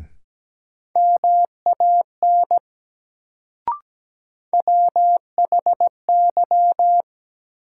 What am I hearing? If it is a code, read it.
Morse code, heard as a keyed tone, about 17 wpm, 700 Hz. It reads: MAN WHY
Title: Sentences from Top 200 Words - New Word Drill - 17wpm